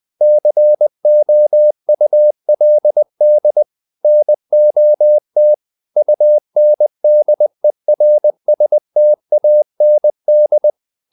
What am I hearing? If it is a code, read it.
Morse code, heard as a keyed tone, about 20 wpm, 600 Hz. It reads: COULD NOT UNDERSTAND
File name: the old man and the sea / the old man and the sea part 1 20wpm